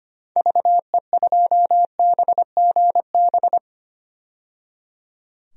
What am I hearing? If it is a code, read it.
Morse code, heard as a keyed tone, about 25 wpm, 700 Hz. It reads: VE2BGB
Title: DXCC-100 Call Signs - 6 Characters - 25wpm